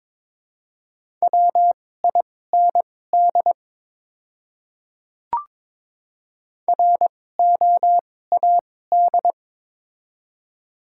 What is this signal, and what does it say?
Morse code, heard as a keyed tone, about 22 wpm, spaced as for 16 wpm, 700 Hz. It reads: WIND ROAD
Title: 2x Character Spacing - Top 500 Words - 22wpm